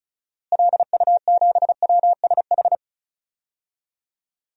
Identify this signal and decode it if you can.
Morse code, heard as a keyed tone, about 36 wpm, 700 Hz. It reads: LU7WSH
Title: DXCC-100 Call Signs - 6 Characters - Repeated 3x - 35wpm